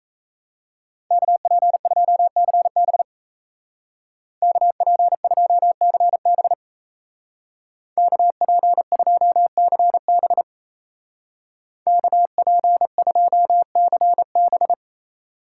Morse code, heard as a keyed tone, about 37 wpm, 700 Hz. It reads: KP2CB KP2CB KP2CB KP2CB